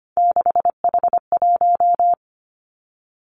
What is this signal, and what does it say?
Morse code, heard as a keyed tone, about 25 wpm, 700 Hz. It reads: 6H1
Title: Three Challenging Letter-Number Combinations 25wpm